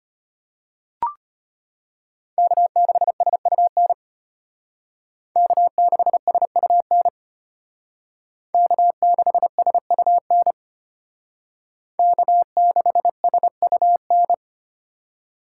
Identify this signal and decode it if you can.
Morse code, heard as a keyed tone, about 32 wpm, 700 Hz. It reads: K6SUN K6SUN K6SUN K6SUN